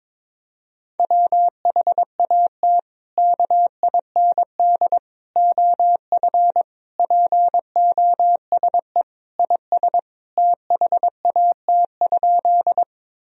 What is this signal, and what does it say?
Morse code, heard as a keyed tone, about 22 wpm, 700 Hz. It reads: WHAT KIND OF POSE IS THAT?
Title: Sentences from Top 1000 Words - Review Part 2 - 22wpm